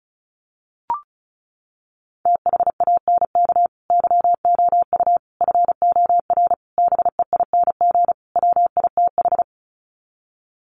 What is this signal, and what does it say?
Morse code, heard as a keyed tone, about 34 wpm, 700 Hz. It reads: THANK YOU FOR BEING WITH
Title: Sets of 5 Words 35wpm